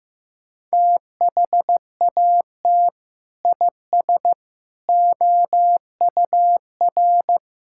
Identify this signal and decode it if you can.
Morse code, heard as a keyed tone, about 15 wpm, 700 Hz. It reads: THAT IS OUR
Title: Sentences from Top 1000 Words - Review Part 5 - 15wpm